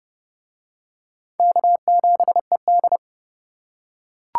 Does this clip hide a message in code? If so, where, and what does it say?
Yes: Morse code from 1.4 to 3.0 s, heard as a keyed tone, about 29 wpm, 700 Hz. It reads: K7ED